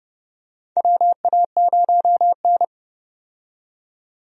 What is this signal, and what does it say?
Morse code, heard as a keyed tone, about 31 wpm, 700 Hz. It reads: WA0N
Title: USA Call Signs - 4 Characters - 30wpm